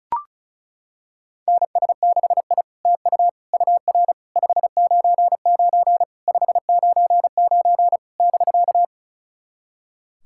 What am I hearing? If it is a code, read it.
Morse code, heard as a keyed tone, about 36 wpm, 690 Hz. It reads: NS6I TU UR 599 599 <BK>
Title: SOTA QSO Practice 35wpm